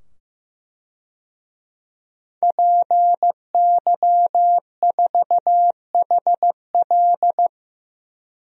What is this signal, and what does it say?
Morse code, heard as a keyed tone, about 15 wpm, 700 Hz. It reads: PY4HL
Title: DXCC-100 Call Signs - 5 Characters - 15wpm